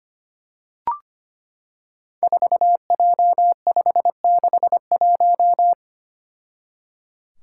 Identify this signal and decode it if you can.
Morse code, heard as a keyed tone, about 25 wpm, 700 Hz. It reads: 4J561
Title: Five Challenging Letter-Number Combinations 25wpm